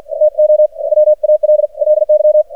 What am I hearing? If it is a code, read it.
Morse code, heard as a keyed tone, about 48 wpm, 610 Hz. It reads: TO WALLY
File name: archived / CRSnet